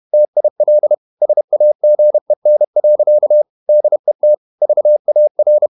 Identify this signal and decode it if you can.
Morse code, heard as a keyed tone, about 31 wpm, 600 Hz. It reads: TIL SAGEN. DET VAR
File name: Ferie QRQ 31wpm